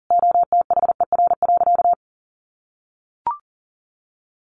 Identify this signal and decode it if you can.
Morse code, heard as a keyed tone, about 40 wpm, 700 Hz. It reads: OTHER.
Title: Sentences from Top 1500 Words 40wpm